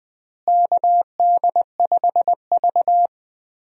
Morse code, heard as a keyed tone, about 21 wpm, 700 Hz. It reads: KD5V
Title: USA Call Signs - 4 Characters - Repeated 3x - 20wpm